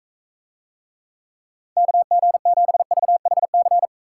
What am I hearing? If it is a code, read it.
Morse code, heard as a keyed tone, about 43 wpm, 700 Hz. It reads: KG7VHC